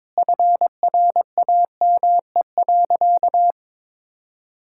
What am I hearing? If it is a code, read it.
Morse code, heard as a keyed tone, about 22 wpm, 700 Hz. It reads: FRAME.
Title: Sentences from Top 1000 Words - Review Part 6 - 22wpm